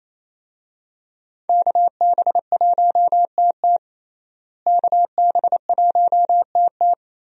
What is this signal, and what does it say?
Morse code, heard as a keyed tone, about 27 wpm, 700 Hz. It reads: KB1TT KB1TT
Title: USA Call Signs - 5 Characters - Repeated 2x - 28wpm